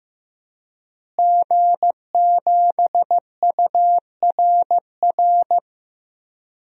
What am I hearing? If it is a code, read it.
Morse code, heard as a keyed tone, about 16 wpm, 700 Hz. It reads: G7URR
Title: DXCC-100 Call Signs - 5 Characters - Repeated 3x - 15wpm